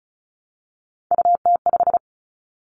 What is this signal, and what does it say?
Morse code, heard as a keyed tone, about 34 wpm, 700 Hz. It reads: UT5
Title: Callsign Prefixes for Top 25 DXCC - Repeated 3x - 35wpm